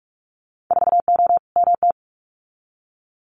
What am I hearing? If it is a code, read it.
Morse code, heard as a keyed tone, about 46 wpm, 700 Hz. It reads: 4O MT
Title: Winter Field Day Exchanges Encore 45wpm